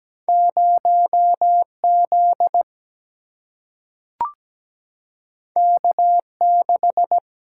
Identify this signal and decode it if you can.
Morse code, heard as a keyed tone, about 17 wpm, 700 Hz. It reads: 0Z K6
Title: Single-Letter Single-Number 17wpm